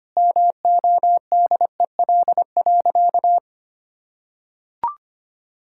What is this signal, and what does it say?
Morse code, heard as a keyed tone, about 25 wpm, 700 Hz. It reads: MODEL.
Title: Sentences from Top 1000 Words - Review Part 5 - 25wpm